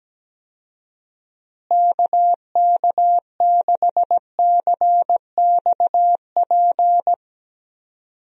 Morse code, heard as a keyed tone, about 17 wpm, 700 Hz. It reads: KK6CXP
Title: USA Call Signs - 6 Characters - Repeated 2x - 17wpm